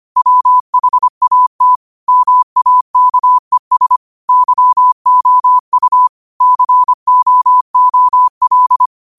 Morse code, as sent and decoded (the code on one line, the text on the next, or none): .-- .... .- - / -- .- -.- . ... / -.-- --- ..- / -.-. --- --- .-..
WHAT MAKES YOU COOL